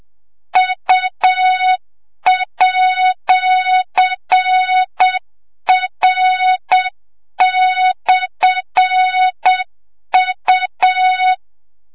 Morse code, heard as a keyed tone, about 7 wpm, 3700 Hz, with noise. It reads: U@R/U
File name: Morze-HA2MN / hang / 25 GYAKORLAT